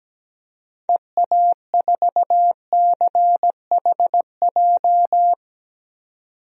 Morse code, heard as a keyed tone, about 17 wpm, 700 Hz. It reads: EA4CHJ
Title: DXCC-100 Call Signs - 6 Characters - 17wpm